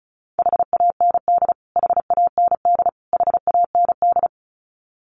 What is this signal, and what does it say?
Morse code, heard as a keyed tone, about 34 wpm, 700 Hz. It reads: HAND HAND HAND
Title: Top 200 Words Repeated 35wpm